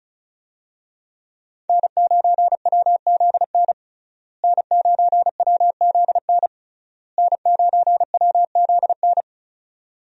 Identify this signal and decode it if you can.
Morse code, heard as a keyed tone, about 34 wpm, 700 Hz. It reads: N9WZN N9WZN N9WZN
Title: USA Call Signs - 5 Characters - Repeated 3x - 35wpm